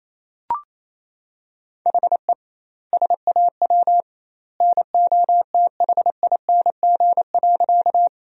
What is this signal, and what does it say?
Morse code, heard as a keyed tone, about 28 wpm, 700 Hz. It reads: HE SAW NOTHING.
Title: 2x Word-Spacing - Sentences from Top 500 Words - 28wpm